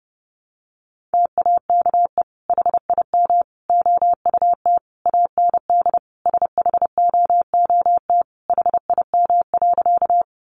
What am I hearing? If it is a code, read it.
Morse code, heard as a keyed tone, about 30 wpm, 700 Hz. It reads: TAKE HIM OUT AND SHOOT HIM.
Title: Sentences from Top 1200 Words 30wpm